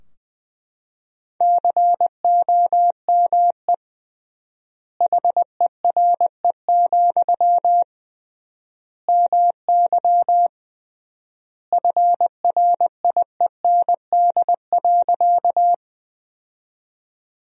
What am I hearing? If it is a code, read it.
Morse code, heard as a keyed tone, about 21 wpm, 700 Hz. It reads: COME HERE, MY FRIEND.
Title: 3x Word-Spacing - Sentences from Top 300 Words - 20wpm